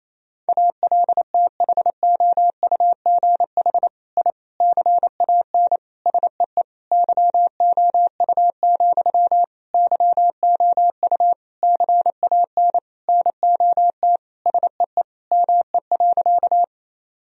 Morse code, heard as a keyed tone, about 28 wpm, 700 Hz. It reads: ALTHOUGH I CAN SEE YOU, YOU CAN NOT SEE ME.
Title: Sentences from Top 1000 Words - Review Part 3 - 28wpm